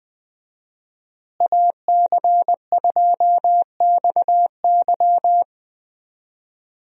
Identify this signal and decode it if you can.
Morse code, heard as a keyed tone, about 20 wpm, 700 Hz. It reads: AC2XY
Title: USA Call Signs - 5 Characters - Repeated 2x - 20wpm